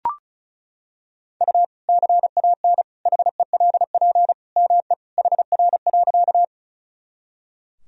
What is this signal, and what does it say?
Morse code, heard as a keyed tone, about 36 wpm, 700 Hz. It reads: U CAN HELP ME HR.
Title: Sentences from Top 200 Words and Abbreviations 35wpm